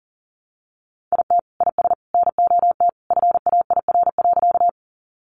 Morse code, heard as a keyed tone, about 40 wpm, 700 Hz. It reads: IT IS NOT FAIR.
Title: Sentences from Top 1300 Words 40wpm